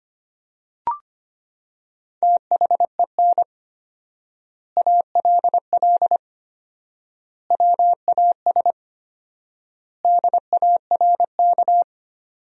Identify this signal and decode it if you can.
Morse code, heard as a keyed tone, about 25 wpm, 700 Hz. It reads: THEN ALL WAS DARK
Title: 4x Word-Spacing - Sentences from Top 500 Words - 25wpm